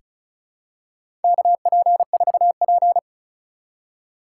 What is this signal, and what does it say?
Morse code, heard as a keyed tone, about 36 wpm, 690 Hz. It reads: KP4P
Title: USA Call Signs - 4 Characters - 3x Character Spacing - 35wpm